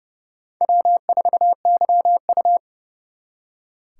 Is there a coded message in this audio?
Yes: Morse code from 0.6 to 2.6 s, heard as a keyed tone, about 38 wpm, 700 Hz. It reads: W4YU